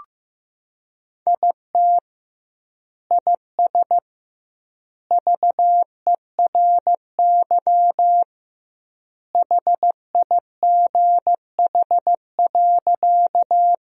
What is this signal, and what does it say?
Morse code, heard as a keyed tone, about 15 wpm, 700 Hz. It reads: IT IS VERY HIGH.
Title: 2x Word-Spacing - Sentences from Top 200 Words - 15wpm